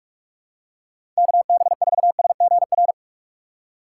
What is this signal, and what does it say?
Morse code, heard as a keyed tone, about 46 wpm, 700 Hz. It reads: KB4SGR
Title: USA Call Signs - 6 Characters - 2x Character Spacing - 45wpm